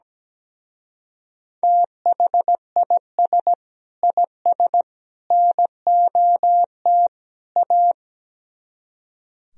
Morse code, heard as a keyed tone, about 17 wpm, 700 Hz. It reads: THIS IS NOT A
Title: Speed-Racing - Sets of 4 Words 17wpm